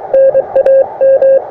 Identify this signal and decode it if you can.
Morse code, heard as a keyed tone, about 22 wpm, 530 Hz. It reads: NAM
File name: attachments / europe 20m